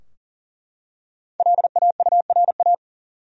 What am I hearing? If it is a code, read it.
Morse code, heard as a keyed tone, about 40 wpm, 700 Hz. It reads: LAURA
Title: USA Common Names - Max 5 Letters - 40wpm